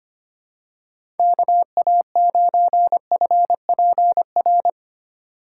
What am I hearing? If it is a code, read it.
Morse code, heard as a keyed tone, about 25 wpm, 700 Hz. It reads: KA9FPR
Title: USA Call Signs - 6 Characters - 3x Character Spacing - 25wpm